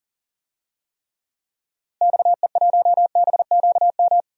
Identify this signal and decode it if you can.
Morse code, heard as a keyed tone, about 39 wpm, 700 Hz. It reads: XE1BQM
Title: DXCC-100 Call Signs - 6 Characters - Repeated 2x - 40wpm